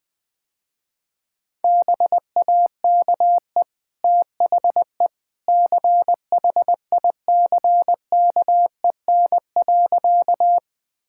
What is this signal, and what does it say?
Morse code, heard as a keyed tone, about 21 wpm, 700 Hz. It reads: BAKE THE CHICKEN.
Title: Easy Sentences with Hints - 20wpm